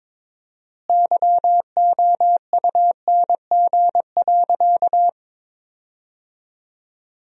Morse code, heard as a keyed tone, about 21 wpm, 700 Hz. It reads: YOUNG.
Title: 4x Word-Spacing - Sentences from Top 400 Words - 22wpm